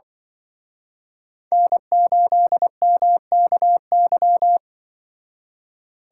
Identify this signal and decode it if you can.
Morse code, heard as a keyed tone, about 25 wpm, 700 Hz. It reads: N8MKY
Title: Speed Racing - USA 1x3 Call Signs 20wpm